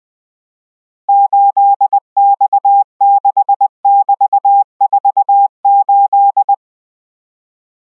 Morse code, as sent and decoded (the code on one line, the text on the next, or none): ---.. -..- -.... -...- ....- ---..
8X6<BT>48